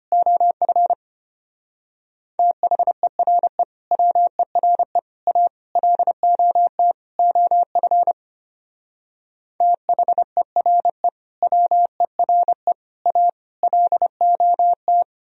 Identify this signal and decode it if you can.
Morse code, heard as a keyed tone, about 30 wpm, 700 Hz. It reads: OF THERE WERE A LOT OF THERE WERE A LOT
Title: Speed-Racing - Sets of 5 Words 25wpm